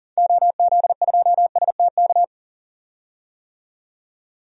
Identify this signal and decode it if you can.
Morse code, heard as a keyed tone, about 40 wpm, 700 Hz. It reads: OZ2STK
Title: DXCC-100 Call Signs - 6 Characters - 40wpm